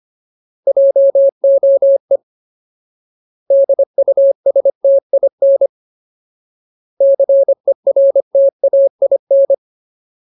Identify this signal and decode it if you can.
Morse code, heard as a keyed tone, about 25 wpm, 550 Hz. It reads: JOE DUSTIN CERTAIN
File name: Ultimate Word List / 25WPM Ultimate